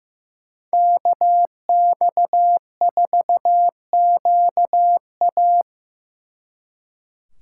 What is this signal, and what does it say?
Morse code, heard as a keyed tone, about 18 wpm, 700 Hz. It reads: KX4QA
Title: Speed Racing - USA 2x2 Call Signs 15wpm